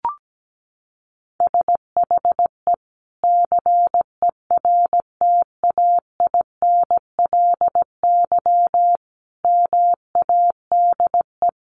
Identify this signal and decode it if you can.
Morse code, heard as a keyed tone, about 17 wpm, 700 Hz. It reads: SHE CERTAINLY MADE
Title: Sentences from Top 1100 Words 17wpm